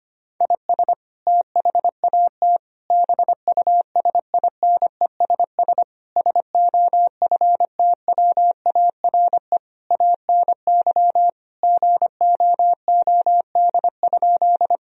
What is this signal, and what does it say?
Morse code, heard as a keyed tone, about 25 wpm, 700 Hz. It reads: IS THAT BUSINESS SOFTWARE ANY GOOD?
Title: Sentences from Top 1000 Words - Review Part 3 - 25wpm